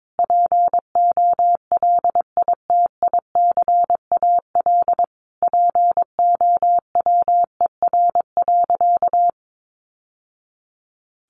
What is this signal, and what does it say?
Morse code, heard as a keyed tone, about 22 wpm, 700 Hz. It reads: POLITICAL POWER.
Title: Sentences from Top 900 Words 22wpm